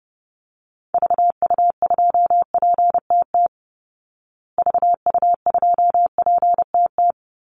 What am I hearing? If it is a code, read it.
Morse code, heard as a keyed tone, about 30 wpm, 700 Hz. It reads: VU2PTT VU2PTT
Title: Call Signs Repeated 3x - 30wpm